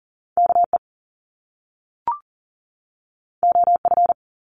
Morse code, heard as a keyed tone, about 38 wpm, 700 Hz. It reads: KE OF
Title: Top Two Letter Combinations in Words 40wpm